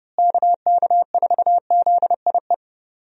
Morse code, heard as a keyed tone, about 32 wpm, 700 Hz. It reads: KK4ZIE
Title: USA Call Signs - 6 Characters - Repeated 3x - 30wpm